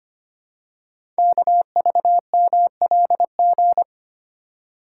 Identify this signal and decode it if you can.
Morse code, heard as a keyed tone, about 25 wpm, 700 Hz. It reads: KVMLG
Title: Code Groups - 5 Letters - 25wpm